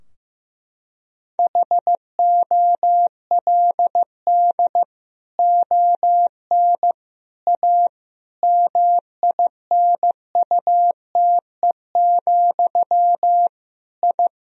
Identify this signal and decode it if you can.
Morse code, heard as a keyed tone, about 16 wpm, 700 Hz. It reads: HOLD ON A MINUTE, I
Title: Sentences from Top 1000 Words - Review Part 2 - 15wpm